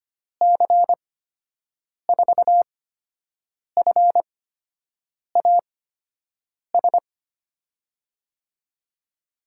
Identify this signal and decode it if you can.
Morse code, heard as a keyed tone, about 25 wpm, spaced as for 7 wpm, 700 Hz. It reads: C4FAS
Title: USA Call Signs - 6 Characters - 8x Character Spacing - 25wpm